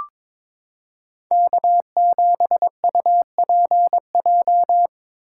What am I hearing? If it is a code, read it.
Morse code, heard as a keyed tone, about 21 wpm, 700 Hz. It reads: K7UPJ
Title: K1USN SST QSO Practice - 22wpm